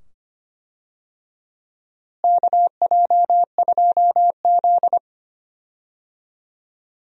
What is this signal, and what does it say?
Morse code, heard as a keyed tone, about 25 wpm, 700 Hz. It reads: KJ2Z
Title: USA Call Signs - 4 Characters - Repeated 2x - 25wpm